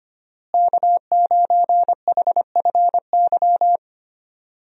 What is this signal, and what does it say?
Morse code, heard as a keyed tone, about 25 wpm, 690 Hz. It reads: K9HFY
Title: USA Call Signs - 5 Characters - Repeated 2x - 25wpm